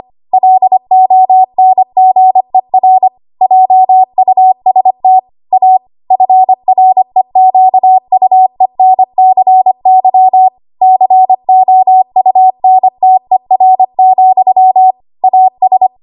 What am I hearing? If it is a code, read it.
Morse code, heard as a keyed tone, about 25 wpm, 750 Hz. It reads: LONGER JUST A FREQUENCY COUNTER, AS
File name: Archive / 25 WPM / 25WPM